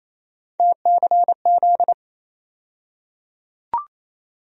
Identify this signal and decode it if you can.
Morse code, heard as a keyed tone, about 28 wpm, 700 Hz. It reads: TCZ
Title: USA Call Sign Postfixes - 3 Letters 28wpm